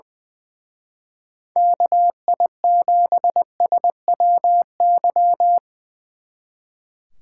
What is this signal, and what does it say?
Morse code, heard as a keyed tone, about 22 wpm, 700 Hz. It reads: KI7SWY
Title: Speed Racing - USA 2x3 Call Signs 20wpm